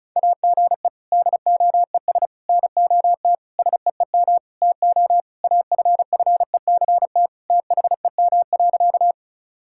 Morse code, heard as a keyed tone, about 36 wpm, 700 Hz. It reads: AGE DOES NOT SEEM TO AFFECT THEM.
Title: Sentences from Top 1000 Words - Review Part 6 - 35wpm